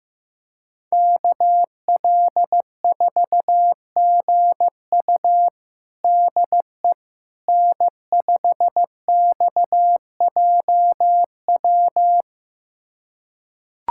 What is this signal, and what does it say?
Morse code, heard as a keyed tone, about 16 wpm, 700 Hz. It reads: KL4GU DE N5XJW
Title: USA Callsign DE Callsign - 5 Characters - 15wpm